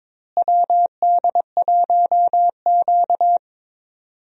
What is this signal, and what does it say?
Morse code, heard as a keyed tone, about 22 wpm, 700 Hz. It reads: WD1Q
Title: USA Call Signs - 4 Characters - Repeated 2x - 22wpm